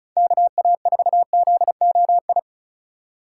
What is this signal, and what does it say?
Morse code, heard as a keyed tone, about 47 wpm, 700 Hz. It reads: KA4ZOI